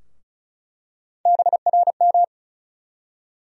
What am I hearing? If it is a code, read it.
Morse code, heard as a keyed tone, about 36 wpm, 700 Hz. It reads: BRM
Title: USA Call Sign Postfixes - 3 Letters 35wpm